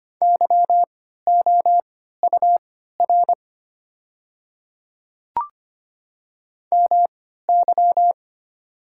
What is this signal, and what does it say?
Morse code, heard as a keyed tone, about 25 wpm, spaced as for 14 wpm, 700 Hz. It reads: YOUR MY
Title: 3x Character Spacing - Top 100 Words - 25wpm